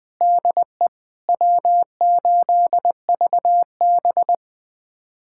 Morse code, heard as a keyed tone, about 20 wpm, 700 Hz. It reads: DE W8VB
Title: USA Callsign DE Callsign - 4 Characters - 20wpm